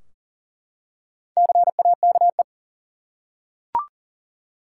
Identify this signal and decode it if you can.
Morse code, heard as a keyed tone, about 37 wpm, 700 Hz. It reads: CAKE
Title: Four Letter Words - 40wpm